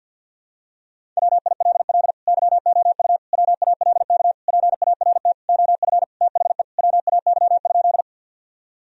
Morse code, heard as a keyed tone, about 51 wpm, 700 Hz. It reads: WILL YOU WALK PART OF THE WAY?
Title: Sentences from Top 1000 Words - Review Part 1 - 50wpm